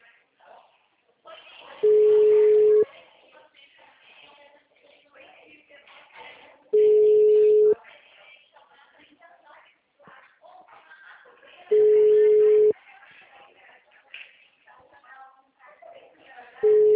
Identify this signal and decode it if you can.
Ringback tone (Europe, 425 Hz), with speech.